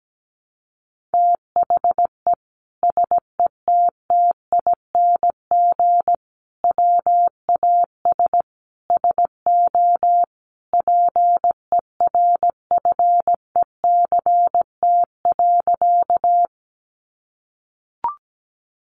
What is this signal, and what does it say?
Morse code, heard as a keyed tone, about 17 wpm, 700 Hz. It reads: THE SETTING WAS SO PERFECT.
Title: Sentences from Top 1300 Words 17wpm